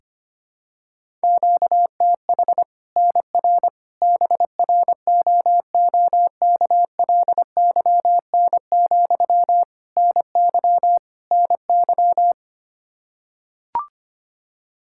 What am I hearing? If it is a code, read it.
Morse code, heard as a keyed tone, about 25 wpm, 700 Hz. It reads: QTH NR BROOKLYN, NY NY
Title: Practice QSO - Exchange 1 - Beginning - 25wpm